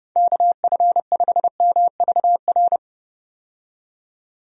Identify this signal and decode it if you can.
Morse code, heard as a keyed tone, about 30 wpm, 700 Hz. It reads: KF5MVR
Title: USA Call Signs - 6 Characters - 30wpm